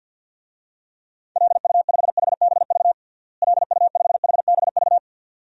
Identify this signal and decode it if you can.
Morse code, heard as a keyed tone, about 53 wpm, 700 Hz. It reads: LU5HBV LU5HBV
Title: DXCC-100 Call Signs - 6 Characters - Repeated 2x - 50wpm